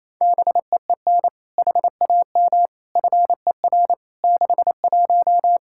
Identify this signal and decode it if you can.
Morse code, heard as a keyed tone, about 28 wpm, 700 Hz. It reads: BEEN HAM FER 61
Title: Practice QSO - Exchange 3 - More Info - 28wpm